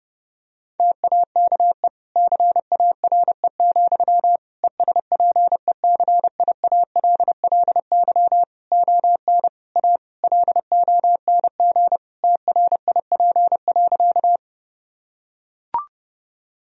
Morse code, heard as a keyed tone, about 32 wpm, 700 Hz. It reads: TAKE CARE, ESPECIALLY ON A LONG TRIP.
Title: Sentences from Top 1000 Words - Review Part 4 - 30wpm